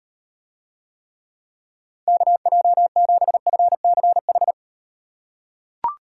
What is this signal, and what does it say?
Morse code, heard as a keyed tone, about 34 wpm, 700 Hz. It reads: KJ7FCH